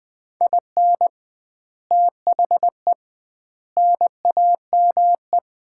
Morse code, heard as a keyed tone, about 21 wpm, 700 Hz. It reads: IN THE NAME
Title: 2x Word-Spacing - Sets of 4 Words - 20wpm